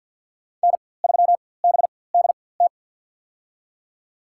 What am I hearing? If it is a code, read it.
Morse code, heard as a keyed tone, about 52 wpm, spaced as for 23 wpm, 700 Hz. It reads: N3BDT
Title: USA Call Signs - 5 Characters - 4x Character Spacing - 50wpm